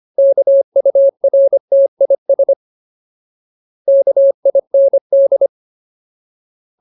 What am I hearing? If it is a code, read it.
Morse code, heard as a keyed tone, about 25 wpm, 550 Hz. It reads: KURTIS KIND
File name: Ultimate Word List / 25WPM Ultimate